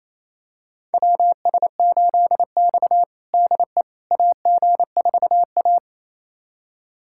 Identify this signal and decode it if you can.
Morse code, heard as a keyed tone, about 28 wpm, 700 Hz. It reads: WS8X DE AG4A
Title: USA Callsign DE Callsign - 4 Characters - 28wpm